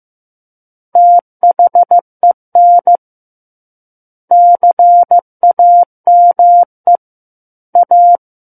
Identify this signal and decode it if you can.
Morse code, heard as a keyed tone, about 15 wpm, 700 Hz. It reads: THEN CAME A
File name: Book One / 06-Chapter 6 - 15 WPM